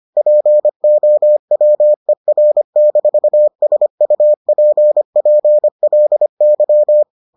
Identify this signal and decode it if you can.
Morse code, heard as a keyed tone, about 25 wpm, 600 Hz. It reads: POWER-SUPPLY